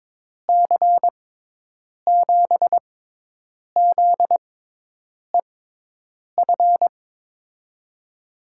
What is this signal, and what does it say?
Morse code, heard as a keyed tone, about 21 wpm, spaced as for 8 wpm, 700 Hz. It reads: C7ZEF
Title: USA Call Signs - 6 Characters - 6x Character Spacing - 22wpm